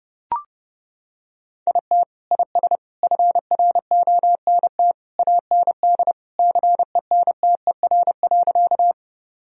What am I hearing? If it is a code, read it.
Morse code, heard as a keyed tone, about 30 wpm, 700 Hz. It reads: IT IS FRONT AND CENTER.
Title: Sentences from Top 1000 Words - Review Part 2 - 30wpm